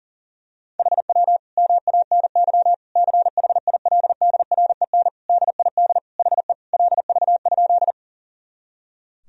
Morse code, heard as a keyed tone, about 40 wpm, 700 Hz. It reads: HW MANY CHILDREN DID HE LV?
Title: Sentences from Top 400 Words and Abbreviations 40wpm